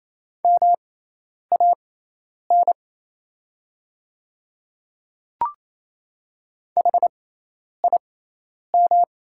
Morse code, heard as a keyed tone, about 28 wpm, spaced as for 10 wpm, 700 Hz. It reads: MAN HIM
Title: 6x Character Spacing - Top 200 Words - 28wpm